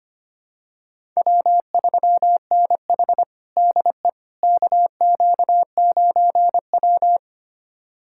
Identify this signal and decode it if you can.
Morse code, heard as a keyed tone, about 25 wpm, 700 Hz. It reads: W3NH DE KQ9W
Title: USA Callsign DE Callsign - 4 Characters - 25wpm